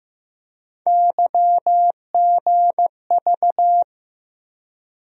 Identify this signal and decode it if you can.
Morse code, heard as a keyed tone, about 15 wpm, 700 Hz. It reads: YGV
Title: USA Call Sign Postfixes - 3 Letters 15wpm